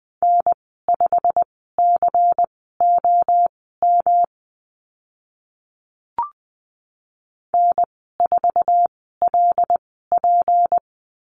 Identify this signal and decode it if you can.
Morse code, heard as a keyed tone, about 20 wpm, spaced as for 15 wpm, 700 Hz. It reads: N5COM N4LP
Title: 161 Lesson 26g - Letter g - Call Signs 20wpm